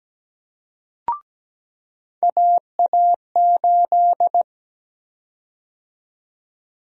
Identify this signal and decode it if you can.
Morse code, heard as a keyed tone, about 17 wpm, 700 Hz. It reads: AA8
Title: USA Call Sign Prefixes - 17wpm